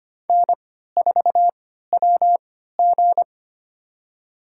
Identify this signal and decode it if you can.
Morse code, heard as a keyed tone, about 25 wpm, spaced as for 14 wpm, 700 Hz. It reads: N4WG
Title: USA Call Signs - 4 Characters - 3x Character Spacing - 25wpm